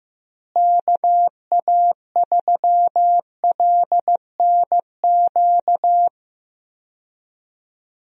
Morse code, heard as a keyed tone, about 15 wpm, 700 Hz. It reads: KA3LNQ
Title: USA Call Signs - 6 Characters - Repeated 2x - 15wpm